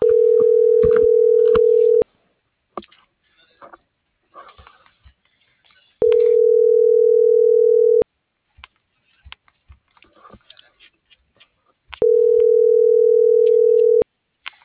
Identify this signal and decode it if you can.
Ringback tone (North America), 440 + 480 Hz.